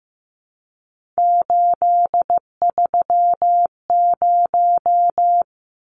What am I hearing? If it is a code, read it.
Morse code, heard as a keyed tone, about 15 wpm, 700 Hz. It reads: 830
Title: Numbers - 3-Digit 15wpm